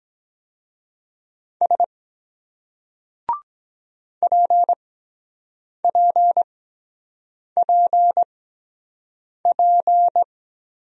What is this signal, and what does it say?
Morse code, heard as a keyed tone, about 23 wpm, spaced as for 7 wpm, 700 Hz. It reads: S PPPP